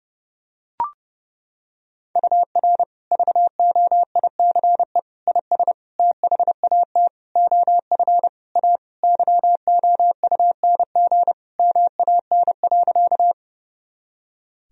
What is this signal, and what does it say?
Morse code, heard as a keyed tone, about 32 wpm, 700 Hz. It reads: UR VOICE IS THAT OF A YOUNG MAN.
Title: Sentences from Top 500 Words and Abbreviations 30wpm